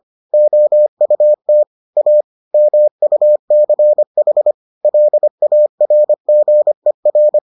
Morse code, heard as a keyed tone, about 25 wpm, 600 Hz. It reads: OUT A MUCH LARGER